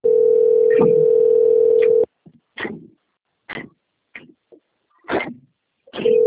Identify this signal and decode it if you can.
Ringback tone (North America), 440 + 480 Hz.